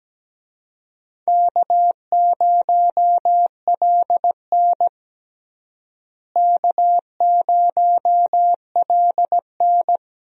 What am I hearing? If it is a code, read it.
Morse code, heard as a keyed tone, about 17 wpm, 700 Hz. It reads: K0LN K0LN
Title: USA Call Signs - 4 Characters - Repeated 3x - 17wpm